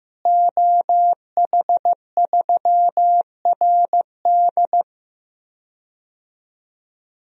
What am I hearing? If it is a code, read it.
Morse code, heard as a keyed tone, about 15 wpm, 700 Hz. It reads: OH3RD
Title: DXCC-100 Call Signs - 5 Characters - Repeated 3x - 15wpm